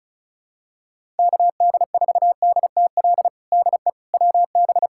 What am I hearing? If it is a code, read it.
Morse code, heard as a keyed tone, about 36 wpm, 700 Hz. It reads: KD4DTL DE WB
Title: USA Callsign DE Callsign - 6 Characters - 35wpm